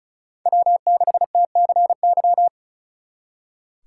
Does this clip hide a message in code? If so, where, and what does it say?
Yes: Morse code from 0.5 to 2.5 s, heard as a keyed tone, about 45 wpm, 700 Hz. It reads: W6TCY